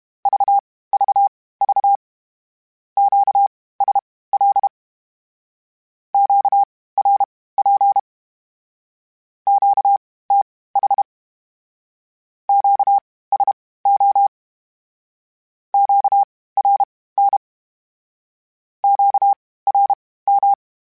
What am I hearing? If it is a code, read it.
Morse code, heard as a keyed tone, about 32 wpm, spaced as for 18 wpm, 800 Hz. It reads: VVV QSL QRP QTH QSO QRN QRM